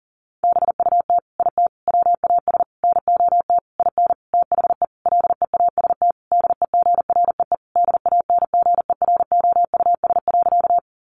Morse code, heard as a keyed tone, about 37 wpm, 700 Hz. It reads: BUT IT WAS NOT IN THE LEAST DEGREE DANGEROUS.
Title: Sentences from Top 1500 Words 40wpm